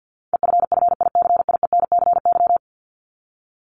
Morse code, heard as a keyed tone, about 50 wpm, 700 Hz. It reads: EFFICIENCY
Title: Top 1501-1750 Words 50wpm